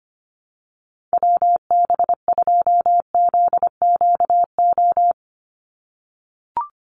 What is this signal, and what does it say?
Morse code, heard as a keyed tone, about 25 wpm, 700 Hz. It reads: WB2ZQO
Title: USA Call Signs 25wpm